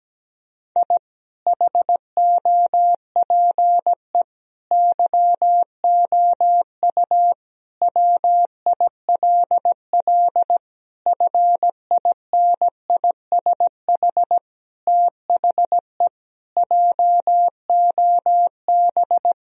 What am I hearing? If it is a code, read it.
Morse code, heard as a keyed tone, about 17 wpm, 700 Hz. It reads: I HOPE YOU WILL FINISH THE JOB
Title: Sentences from Top 1000 Words - Review Part 3 - 17wpm